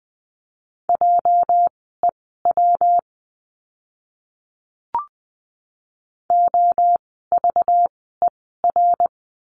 Morse code, heard as a keyed tone, about 20 wpm, spaced as for 14 wpm, 700 Hz. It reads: JEW OVER
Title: 206 Lesson 33f - Letter j - Words 20wpm